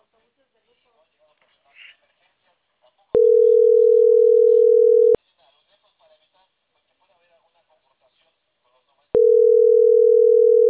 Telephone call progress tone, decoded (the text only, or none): ringback tone